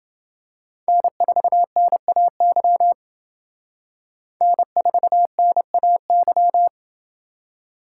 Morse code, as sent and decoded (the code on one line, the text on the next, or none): -. ....- -. .- -.-- / -. ....- -. .- -.--
N4NAY N4NAY